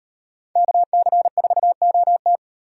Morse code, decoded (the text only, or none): KC4OT